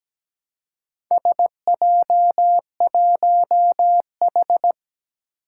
Morse code, heard as a keyed tone, about 17 wpm, 700 Hz. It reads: SJ1H
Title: Four Challenging Letter-Number Combinations 17wpm